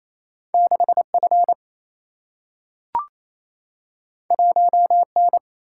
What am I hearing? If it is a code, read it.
Morse code, heard as a keyed tone, about 28 wpm, 700 Hz. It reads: 6 F 1 N
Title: Single-Letter Single-Number 28wpm